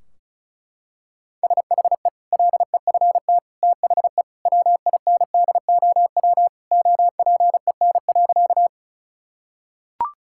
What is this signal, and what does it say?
Morse code, heard as a keyed tone, about 36 wpm, 700 Hz. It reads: SHE LEFT THE WINDOW OPEN.
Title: Sentences from Top 1600 Words 35wpm